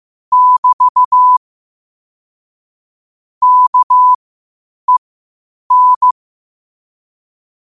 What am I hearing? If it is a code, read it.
Morse code, heard as a keyed tone, about 15 wpm, spaced as for 8 wpm, 1000 Hz. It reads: <BT> KEN
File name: qotdfiles / 8wpm